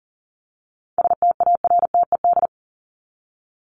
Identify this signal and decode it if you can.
Morse code, heard as a keyed tone, about 40 wpm, 700 Hz. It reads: STARTED
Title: ED Suffix - Top 100 Words with Base Words 40wpm